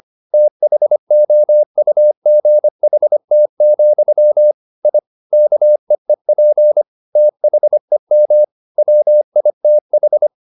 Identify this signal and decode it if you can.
Morse code, heard as a keyed tone, about 25 wpm, 600 Hz. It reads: THOUGHT, I KEEP THEM WITH
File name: the old man and the sea part 1 25wpm